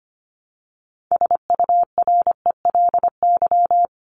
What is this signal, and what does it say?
Morse code, heard as a keyed tone, about 25 wpm, 700 Hz. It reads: SURELY
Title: LY Suffix - Top 100 Words with Base Words 25wpm